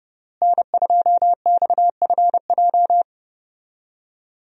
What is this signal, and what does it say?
Morse code, heard as a keyed tone, about 30 wpm, 700 Hz. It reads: N2XFJ
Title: USA Call Signs - 5 Characters - Repeated 3x - 30wpm